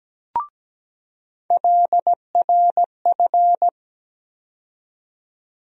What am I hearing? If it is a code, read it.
Morse code, heard as a keyed tone, about 17 wpm, 700 Hz. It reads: LRF
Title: USA Call Sign Postfixes - 3 Letters 17wpm